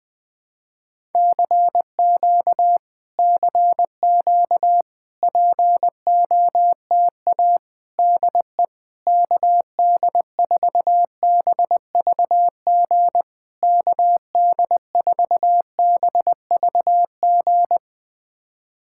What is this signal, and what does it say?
Morse code, heard as a keyed tone, about 20 wpm, 700 Hz. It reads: CQ CQ POTA DE KD4BVG KD4BVG
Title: POTA QSO Practice - 20wpm